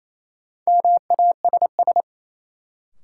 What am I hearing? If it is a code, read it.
Morse code, heard as a keyed tone, about 28 wpm, 700 Hz. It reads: MASS
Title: Sets of 5 Words Encore - New Word Drill - 28wpm